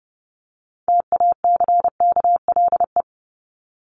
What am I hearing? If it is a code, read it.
Morse code, heard as a keyed tone, about 30 wpm, 700 Hz. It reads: TACKLE
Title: Top 1751-2000 Words 30wpm